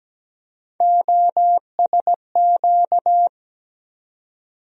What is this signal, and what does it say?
Morse code, heard as a keyed tone, about 17 wpm, 700 Hz. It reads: OSQ
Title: USA Call Sign Postfixes - 3 Letters 17wpm